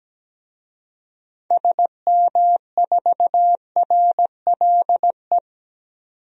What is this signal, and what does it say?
Morse code, heard as a keyed tone, about 17 wpm, 700 Hz. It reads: SM4RLE
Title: DXCC-100 Call Signs - 6 Characters - 17wpm